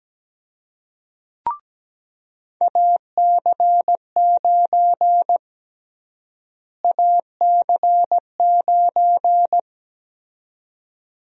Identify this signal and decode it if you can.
Morse code, heard as a keyed tone, about 17 wpm, 700 Hz. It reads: AC9 AC9
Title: USA Call Sign Prefixes - Repeated 2x - 17wpm